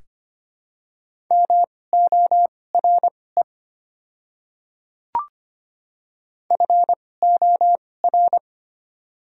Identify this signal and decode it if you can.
Morse code, heard as a keyed tone, about 25 wpm, spaced as for 18 wpm, 700 Hz. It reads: MORE FOR
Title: 2x Character Spacing - Top 100 Words - 25wpm